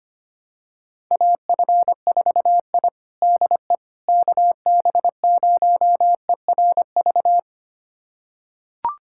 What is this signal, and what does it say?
Morse code, heard as a keyed tone, about 25 wpm, 700 Hz. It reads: AF4I DE KB0ERV
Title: USA Callsign DE Callsign 25wpm